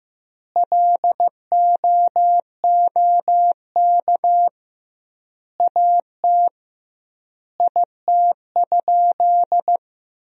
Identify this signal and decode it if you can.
Morse code, heard as a keyed tone, about 16 wpm, 700 Hz. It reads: LOOK AT IT?
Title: 2x Word-Spacing - Sentences from Top 100 Words - 15wpm